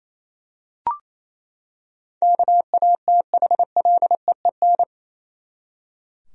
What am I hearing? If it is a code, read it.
Morse code, heard as a keyed tone, about 28 wpm, 700 Hz. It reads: KATHLEEN
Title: USA Common Names - 28wpm